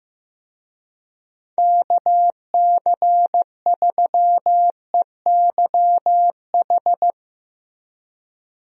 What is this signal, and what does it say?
Morse code, heard as a keyed tone, about 15 wpm, 700 Hz. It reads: KC3EYH
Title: USA Call Signs - 6 Characters - Repeated 2x - 15wpm